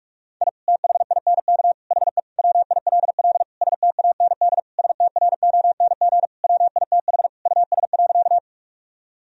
Morse code, heard as a keyed tone, about 41 wpm, 700 Hz. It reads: I THINK HE WILL STAND STRONG WITH US.
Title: Sentences from Top 1000 Words - Review Part 3 - 45wpm